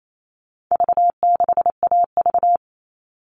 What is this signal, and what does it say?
Morse code, heard as a keyed tone, about 28 wpm, 700 Hz. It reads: V6AV
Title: All Cut Numbers - 4-Digit 28wpm